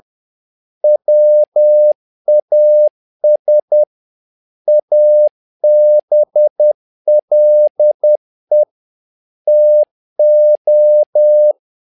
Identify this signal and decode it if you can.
Morse code, heard as a keyed tone, about 10 wpm, 600 Hz. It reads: WAS ABLE TO
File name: learning center / tower trouble / tower trouble 10wpm